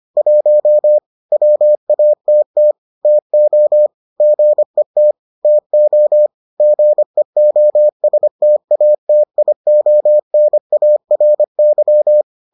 Morse code, heard as a keyed tone, about 25 wpm, 600 Hz. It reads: 1 WATT TO GET TO GEOSTATIONARY